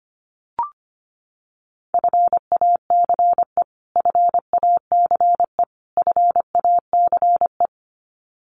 Morse code, heard as a keyed tone, about 25 wpm, 700 Hz. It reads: FACE FACE FACE
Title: Top 500 Words Repeated 25wpm